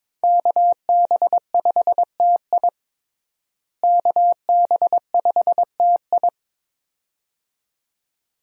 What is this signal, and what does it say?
Morse code, heard as a keyed tone, about 22 wpm, 700 Hz. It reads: KB5TI KB5TI
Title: USA Call Signs - 5 Characters - Repeated 2x - 22wpm